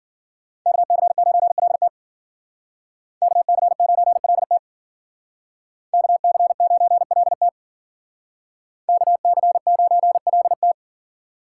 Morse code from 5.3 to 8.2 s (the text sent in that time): KC9LT